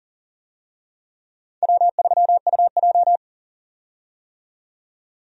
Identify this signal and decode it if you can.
Morse code, heard as a keyed tone, about 40 wpm, 700 Hz. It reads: W3UJ
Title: USA Call Signs - 4 Characters - 40wpm